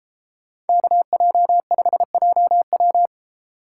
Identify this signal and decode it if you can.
Morse code, heard as a keyed tone, about 32 wpm, 700 Hz. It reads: KJ5JW